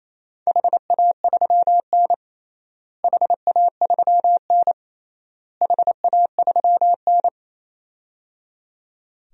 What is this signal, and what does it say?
Morse code, heard as a keyed tone, about 28 wpm, 700 Hz. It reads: HA3N HA3N HA3N
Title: DXCC-100 Call Signs - 4 Characters - Repeated 3x - 28wpm